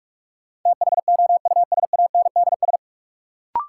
Morse code, heard as a keyed tone, about 46 wpm, 700 Hz. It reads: THOUSANDS
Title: Sets of 4 Words - New Word Drill - 45wpm